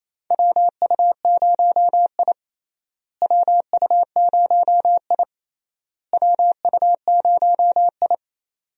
Morse code, heard as a keyed tone, about 28 wpm, 700 Hz. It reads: WU0I WU0I WU0I
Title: USA Call Signs - 4 Characters - Repeated 3x - 28wpm